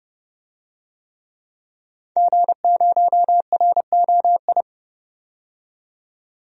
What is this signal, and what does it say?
Morse code, heard as a keyed tone, about 32 wpm, 700 Hz. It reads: G0ROI
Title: DXCC-100 Call Signs - 5 Characters - 30wpm